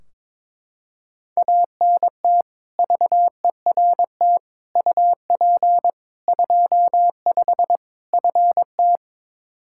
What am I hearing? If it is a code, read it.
Morse code, heard as a keyed tone, about 22 wpm, 700 Hz. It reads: ANT VERT UP 25 FT
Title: Practice QSO - Exchange 2 - Info - 22wpm